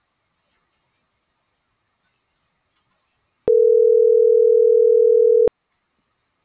Ringback tone (North America), 440 + 480 Hz.